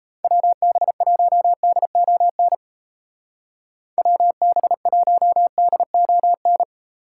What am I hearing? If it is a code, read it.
Morse code, heard as a keyed tone, about 34 wpm, 700 Hz. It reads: WB1DON WB1DON
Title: Speed Racing - USA 2x3 Call Signs 28wpm